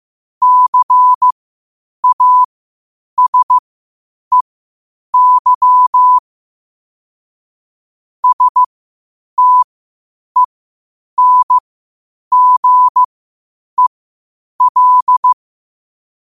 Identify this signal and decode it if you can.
Morse code, heard as a keyed tone, about 15 wpm, spaced as for 8 wpm, 1000 Hz. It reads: CASEY STENGEL
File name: qotdfiles / 8wpm